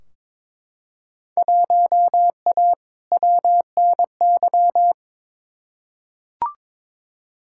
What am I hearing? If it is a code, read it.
Morse code, heard as a keyed tone, about 22 wpm, 700 Hz. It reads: 1A WNY
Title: Field Day Exchanges - 22wpm